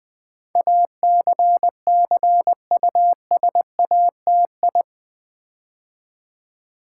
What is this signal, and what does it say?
Morse code, heard as a keyed tone, about 20 wpm, 700 Hz. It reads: ACCUSATI
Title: Build A Word - 4 Letters At A Time - 10-Letter Words - 20wpm